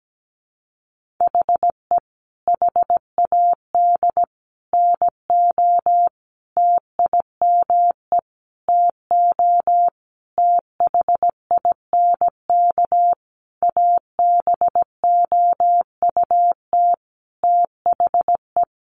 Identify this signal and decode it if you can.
Morse code, heard as a keyed tone, about 17 wpm, 700 Hz. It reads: HE HAD NO TIME TO THINK ABOUT THE
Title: Sentences from Top 700 Words 17wpm